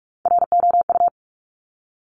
Morse code, heard as a keyed tone, about 46 wpm, 700 Hz. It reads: ROU
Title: Top Three Letter Combinations in Words 45wpm